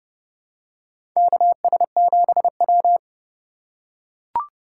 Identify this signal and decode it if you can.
Morse code, heard as a keyed tone, about 32 wpm, 700 Hz. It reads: KS7W
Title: USA Call Signs - 4 Characters - 3x Character Spacing - 30wpm